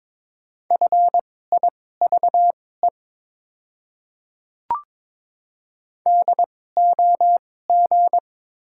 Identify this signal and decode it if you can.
Morse code, heard as a keyed tone, about 21 wpm, spaced as for 16 wpm, 700 Hz. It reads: FIVE DOG
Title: 2x Character Spacing - Top 500 Words - 22wpm